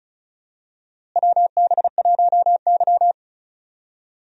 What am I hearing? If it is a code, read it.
Morse code, heard as a keyed tone, about 36 wpm, 700 Hz. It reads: WB1Y
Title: USA Call Signs - 4 Characters - Repeated 2x - 35wpm